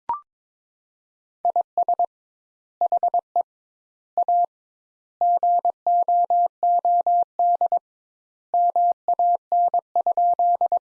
Morse code, heard as a keyed tone, about 22 wpm, 700 Hz. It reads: IS HE A GOOD MAN?
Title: Sentences Top 200 Words - 22wpm